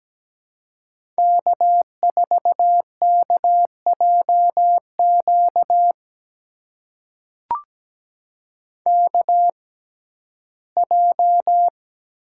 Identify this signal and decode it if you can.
Morse code, heard as a keyed tone, about 17 wpm, spaced as for 12 wpm, 700 Hz. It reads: K4KJQ KJ
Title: USA Call Signs - 6x Character Spacing - 17wpm